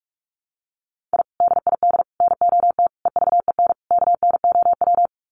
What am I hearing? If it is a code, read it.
Morse code, heard as a keyed tone, about 44 wpm, 700 Hz. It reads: I DID NOT EVEN KNOW
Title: Sets of 5 Words Encore 45wpm